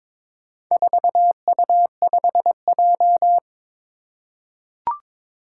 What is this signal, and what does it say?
Morse code, heard as a keyed tone, about 22 wpm, 700 Hz. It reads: 4U5J
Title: Four Challenging Letter-Number Combinations 22wpm